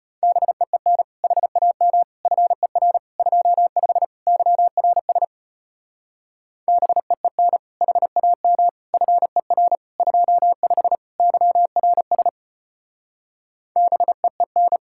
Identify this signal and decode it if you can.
Morse code, heard as a keyed tone, about 33 wpm, 700 Hz. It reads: BEEN HAM FER 25 YRS BEEN HAM FER 25 YRS BEEN